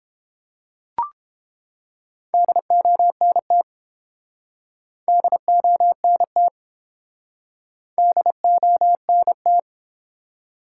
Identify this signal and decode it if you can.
Morse code, heard as a keyed tone, about 29 wpm, 700 Hz. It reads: DONT DONT DONT